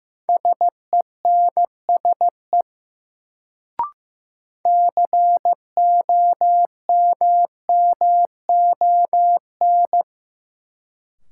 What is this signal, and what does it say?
Morse code, heard as a keyed tone, about 15 wpm, 700 Hz. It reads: SENSE COMMON
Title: Sentences from Top 700 Words - New Word Drill - 15wpm